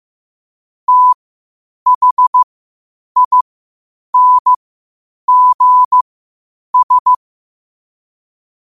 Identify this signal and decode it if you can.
Morse code, heard as a keyed tone, about 15 wpm, spaced as for 8 wpm, 1000 Hz. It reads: THINGS
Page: New quotes every day in morse code at 8 Words per minute.